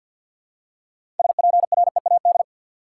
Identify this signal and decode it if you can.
Morse code, heard as a keyed tone, about 47 wpm, 700 Hz. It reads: SPREAD
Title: Six Letter Words - 50wpm